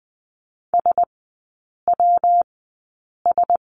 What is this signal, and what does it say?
Morse code, heard as a keyed tone, about 20 wpm, spaced as for 8 wpm, 700 Hz. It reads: SWS
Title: 114 Lesson 19e - Letter w - Three Characters 20wpm